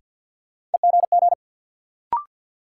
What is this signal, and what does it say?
Morse code, heard as a keyed tone, about 50 wpm, 700 Hz. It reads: EGG
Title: Sentences from Top 1000 Words - Review Part 7 - New Word Drill - 50wpm